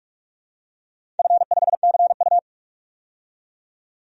Morse code, heard as a keyed tone, about 42 wpm, 700 Hz. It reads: F5CU
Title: DXCC-100 Call Signs - 4 Characters - 45wpm